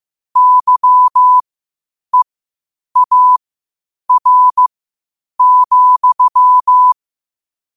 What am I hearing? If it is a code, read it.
Morse code, heard as a keyed tone, about 15 wpm, spaced as for 8 wpm, 1000 Hz. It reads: YEAR,
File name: qotdfiles / 8wpm